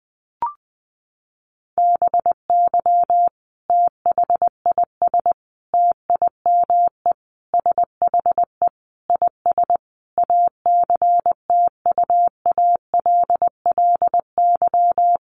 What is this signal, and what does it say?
Morse code, heard as a keyed tone, about 19 wpm, 700 Hz. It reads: BY THIS TIME SHE IS ACTUALLY
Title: Sentences from Top 700 Words 20wpm